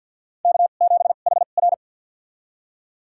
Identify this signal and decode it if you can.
Morse code, heard as a keyed tone, about 51 wpm, spaced as for 36 wpm, 700 Hz. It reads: K7HR
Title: USA Call Signs - 4 Characters - 2x Character Spacing - 50wpm